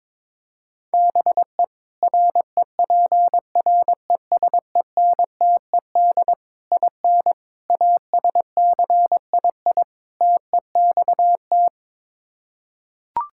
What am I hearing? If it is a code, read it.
Morse code, heard as a keyed tone, about 22 wpm, 700 Hz. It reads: BE REPRESENTED IN ASCII TEXT
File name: practice / 22wpm / Build A Phrase - 5-Word Phrases - 22wpm